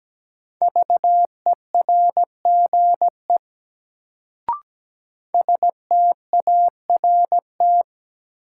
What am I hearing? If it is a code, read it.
Morse code, heard as a keyed tone, about 17 wpm, 700 Hz. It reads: VERGE START
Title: Sets of 4 Words - New Word Drill - 17wpm